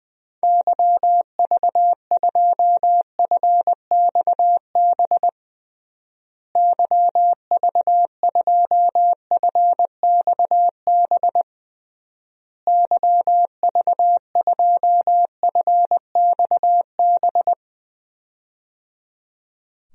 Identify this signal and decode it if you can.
Morse code, heard as a keyed tone, about 20 wpm, 700 Hz. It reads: YV2FXB YV2FXB YV2FXB
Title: DXCC-100 Call Signs - 6 Characters - Repeated 3x - 20wpm